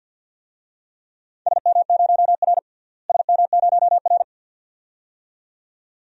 Morse code, heard as a keyed tone, about 50 wpm, 700 Hz. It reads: SM0R SM0R
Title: DXCC-100 Call Signs - 4 Characters - Repeated 2x - 50wpm